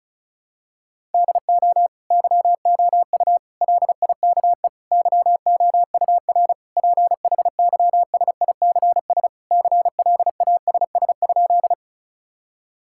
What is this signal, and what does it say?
Morse code, heard as a keyed tone, about 36 wpm, 700 Hz. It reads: DO YOU LIKE YOUR PHYSICS CLASS?
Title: Sentences from Top 1000 Words - Review Part 5 - 35wpm